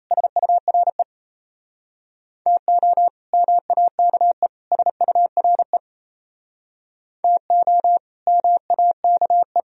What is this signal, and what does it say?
Morse code, heard as a keyed tone, about 35 wpm, 700 Hz. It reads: SURE TO MAKE SURE TO MAKE